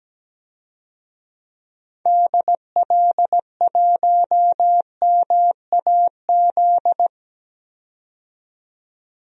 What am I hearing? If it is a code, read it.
Morse code, heard as a keyed tone, about 17 wpm, 700 Hz. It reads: DL1MAZ
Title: DXCC-100 Call Signs - 6 Characters - 17wpm